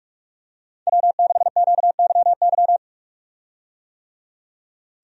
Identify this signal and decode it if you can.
Morse code, heard as a keyed tone, about 46 wpm, 700 Hz. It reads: W6QYY
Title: USA Call Signs - 5 Characters - 45wpm